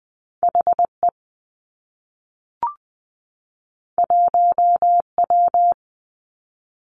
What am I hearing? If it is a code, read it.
Morse code, heard as a keyed tone, about 20 wpm, 700 Hz. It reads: H E 1 W
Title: Two Challenging Letter-Number Combinations 20wpm